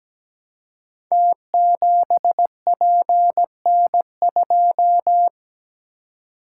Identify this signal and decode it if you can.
Morse code, heard as a keyed tone, about 17 wpm, 700 Hz. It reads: T7PN2
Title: Code Groups - 5 Letter:Numbers - 17wpm